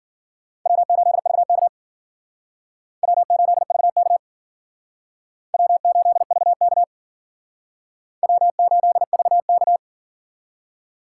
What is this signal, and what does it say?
Morse code, heard as a keyed tone, about 52 wpm, 700 Hz. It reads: W8VK W8VK W8VK W8VK